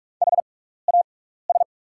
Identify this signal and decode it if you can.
Morse code, heard as a keyed tone, about 44 wpm, spaced as for 15 wpm, 690 Hz. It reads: HAS
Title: 6x Character Spacing - Top 100 Words - 45wpm